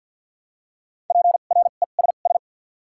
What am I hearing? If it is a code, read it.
Morse code, heard as a keyed tone, about 53 wpm, 700 Hz. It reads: P R E S S
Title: 2x Character Spacing - Top 300 Words - 50wpm